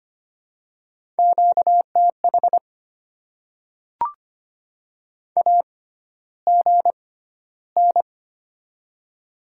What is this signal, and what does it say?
Morse code, heard as a keyed tone, about 25 wpm, spaced as for 13 wpm, 700 Hz. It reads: QTH AGN
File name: Top 20 QSO Elements - 6x Character Spacing - 25wpm